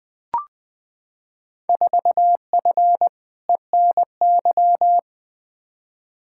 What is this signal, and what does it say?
Morse code, heard as a keyed tone, about 20 wpm, 700 Hz. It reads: 4F ENY
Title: Field Day Exchanges - 20wpm